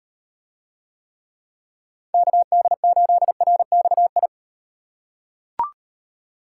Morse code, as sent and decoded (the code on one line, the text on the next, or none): -.- -.. ---.. .-. -..- ..
KD8RXI